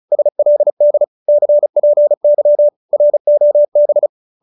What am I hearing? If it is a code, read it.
Morse code, heard as a keyed tone, about 34 wpm, 570 Hz. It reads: SLD CPY ROB